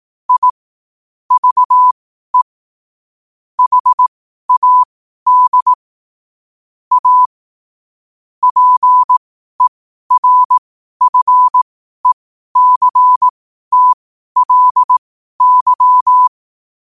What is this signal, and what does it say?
Morse code, heard as a keyed tone, about 18 wpm, spaced as for 12 wpm, 1000 Hz. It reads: I VE HAD A PERFECTLY
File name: qotdfiles / 12wpm